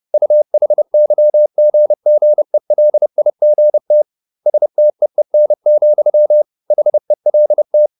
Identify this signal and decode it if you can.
Morse code, heard as a keyed tone, about 30 wpm, 600 Hz. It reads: UHYGGELIGT STEEN, HELT
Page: Ferie 30 wpm